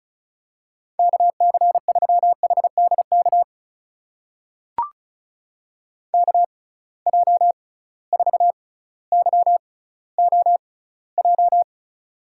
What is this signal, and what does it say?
Morse code, heard as a keyed tone, about 36 wpm, spaced as for 18 wpm, 700 Hz. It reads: KC3HDK KJ4YOJ
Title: USA Call Signs - 6 Characters - 6x Character Spacing - 35wpm